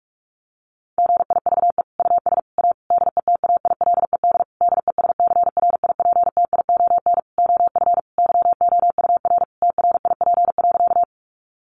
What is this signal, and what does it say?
Morse code, heard as a keyed tone, about 44 wpm, 700 Hz. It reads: GIVE US A DETAILED DESCRIPTION OF YOUR TRIP.
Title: Sentences from Top 1500 Words 45wpm